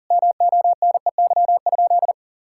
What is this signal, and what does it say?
Morse code, heard as a keyed tone, about 40 wpm, 700 Hz. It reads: MONEY?
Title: Sentences from Top 1000 Words - Review Part 5 - 40wpm